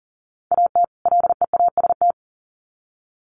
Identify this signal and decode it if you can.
Morse code, heard as a keyed tone, about 40 wpm, 700 Hz. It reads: AT LEAST
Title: Sets of 2 Words 40wpm